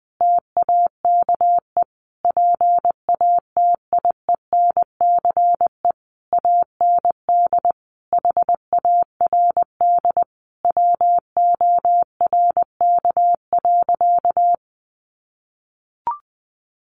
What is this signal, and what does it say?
Morse code, heard as a keyed tone, about 20 wpm, 700 Hz. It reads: TAKE PATIENCE AND HARD WORK.
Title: Sentences from Top 1300 Words 20wpm